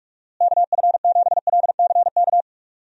Morse code, heard as a keyed tone, about 42 wpm, 700 Hz. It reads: KF7LCK
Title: USA Call Signs - 6 Characters - Repeated 3x - 45wpm